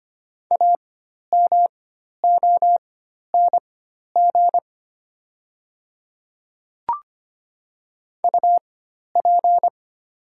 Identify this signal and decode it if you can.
Morse code, heard as a keyed tone, about 25 wpm, spaced as for 12 wpm, 700 Hz. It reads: AMONG UP
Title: 4x Character Spacing - Top 500 Words - 25wpm